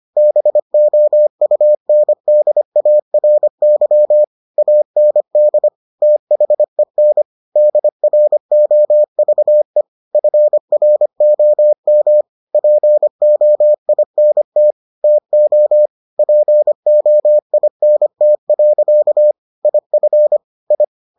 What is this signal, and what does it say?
Morse code, heard as a keyed tone, about 25 wpm, 600 Hz. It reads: BOUNDARY AND THEN DROVE FROM POINT TO POINT. IF I